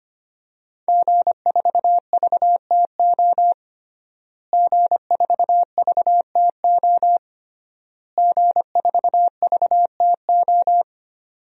Morse code, heard as a keyed tone, about 25 wpm, 700 Hz. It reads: G4VTO G4VTO G4VTO
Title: DXCC-100 Call Signs - 5 Characters - Repeated 3x - 25wpm